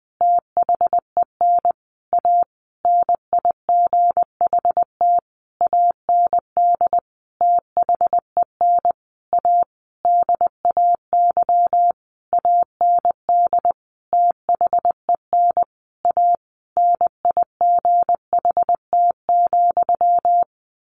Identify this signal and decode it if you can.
Morse code, heard as a keyed tone, about 20 wpm, 700 Hz. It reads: THEN A NIGHT AND THEN A DAY AND THEN A NIGHT,
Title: A Sound Of Thunder 20wpm